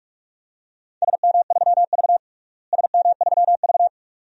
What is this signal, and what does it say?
Morse code, heard as a keyed tone, about 46 wpm, 700 Hz. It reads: SM3V SM3V
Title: DXCC-100 Call Signs - 4 Characters - Repeated 2x - 45wpm